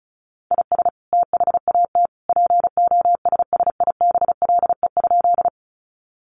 Morse code, heard as a keyed tone, about 34 wpm, 700 Hz. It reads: IS THAT POSSIBLE?
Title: Sentences from Top 500 Words 35wpm